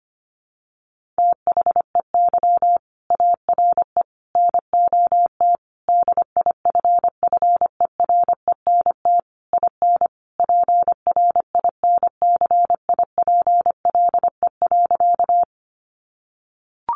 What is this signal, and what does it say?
Morse code, heard as a keyed tone, about 25 wpm, 700 Hz. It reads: THEY ARE NOT DIFFERENT IN PRINCIPLE.
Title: Sentences from Top 1400 Words 25wpm